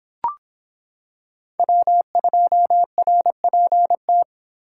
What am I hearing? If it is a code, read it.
Morse code, heard as a keyed tone, about 26 wpm, 700 Hz. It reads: W2RPT